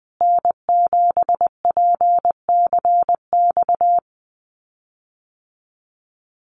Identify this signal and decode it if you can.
Morse code, heard as a keyed tone, about 20 wpm, 700 Hz. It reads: N7PCX
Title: USA Call Signs Repeated 3x 20wpm